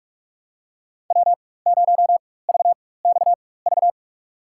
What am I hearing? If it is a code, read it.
Morse code, heard as a keyed tone, about 46 wpm, spaced as for 21 wpm, 700 Hz. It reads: W0VXV
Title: USA Call Signs - 5 Characters - 4x Character Spacing - 45wpm